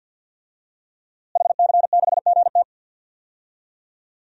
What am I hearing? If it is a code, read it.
Morse code, heard as a keyed tone, about 48 wpm, 700 Hz. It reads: HC6GT
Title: DXCC-100 Call Signs - 5 Characters - Repeated 3x - 50wpm